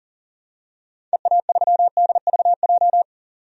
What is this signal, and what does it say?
Morse code, heard as a keyed tone, about 40 wpm, 700 Hz. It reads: EA3DVJ
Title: DXCC-100 Call Signs - 6 Characters - 40wpm